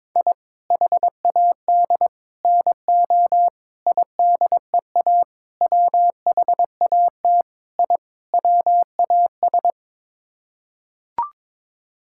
Morse code, heard as a keyed tone, about 22 wpm, 700 Hz. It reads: I HAD NO IDEA WHAT I WAS
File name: practice / 22wpm / Sets of 7 Words - 22wpm